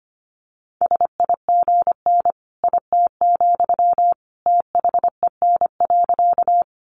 Morse code, heard as a keyed tone, about 25 wpm, 700 Hz. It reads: SIGN IT, THEN.
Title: Sentences from Top 1000 Words 25wpm